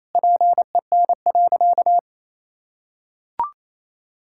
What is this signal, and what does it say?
Morse code, heard as a keyed tone, about 28 wpm, 700 Hz. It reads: PEN.
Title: Pangram Challenge v2 - 28wpm